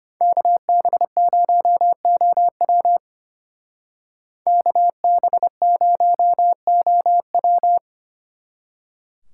Morse code, heard as a keyed tone, about 31 wpm, 700 Hz. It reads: KB0OW KB0OW
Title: Speed Racing - USA 2x2 Call Signs 25wpm